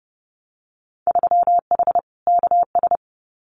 Morse code, heard as a keyed tone, about 30 wpm, 700 Hz. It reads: 3H KS
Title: Winter Field Day Exchanges Encore 30wpm